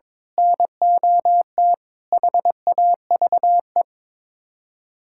Morse code, heard as a keyed tone, about 22 wpm, 700 Hz. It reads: NOT HAVE
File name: practice / 22wpm / Build A Phrase - 5-Word Phrases - 22wpm